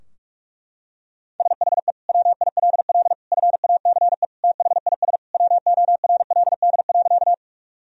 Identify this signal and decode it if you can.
Morse code, heard as a keyed tone, about 46 wpm, 700 Hz. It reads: SHE WILL FACE THIS WORLD.
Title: Sentences from Top 1000 Words - Review Part 2 - 45wpm